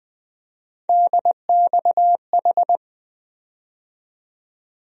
Morse code, heard as a keyed tone, about 20 wpm, 700 Hz. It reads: DXH
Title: USA Call Sign Postfixes - 3 Letters 20wpm